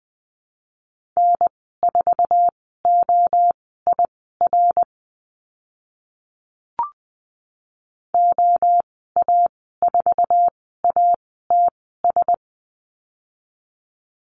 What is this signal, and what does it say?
Morse code, heard as a keyed tone, about 20 wpm, spaced as for 14 wpm, 700 Hz. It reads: N4OIR OA4ATS
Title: 069 Lesson 12g - Letter d - Call Signs 20wpm